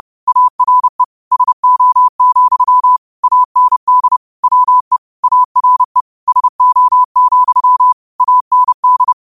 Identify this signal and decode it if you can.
Morse code, heard as a keyed tone, about 30 wpm, 1000 Hz. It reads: ARE SO, AND WE ARE SO, AND
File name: qotdfiles / 30wpm